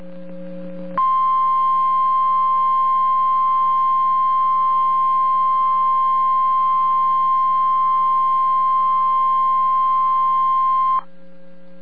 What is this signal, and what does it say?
Emergency Alert System (SAME) tones (the 1050 Hz attention tone). It is heard from 0.9 to 11.1 s.